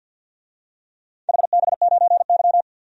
Morse code, heard as a keyed tone, about 50 wpm, 700 Hz. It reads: HB9Y
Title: DXCC-100 Call Signs - 4 Characters - 50wpm